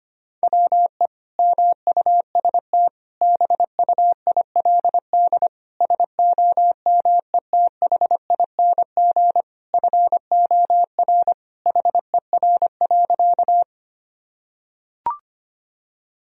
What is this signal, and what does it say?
Morse code, heard as a keyed tone, about 25 wpm, 700 Hz. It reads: WE MUST BUILD SOMETHING FOR HER.
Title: Sentences from Top 1000 Words - Review Part 1 - 25wpm